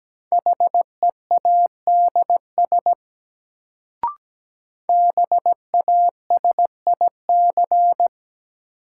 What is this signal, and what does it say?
Morse code, heard as a keyed tone, about 17 wpm, 700 Hz. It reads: HEADS BASIC
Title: Sentences from Top 1000 Words - Review Part 3 - New Word Drill - 17wpm